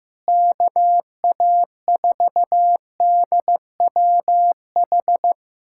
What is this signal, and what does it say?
Morse code, heard as a keyed tone, about 15 wpm, 700 Hz. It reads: KA4DWH
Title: USA Callsign DE Callsign - 6 Characters - 15wpm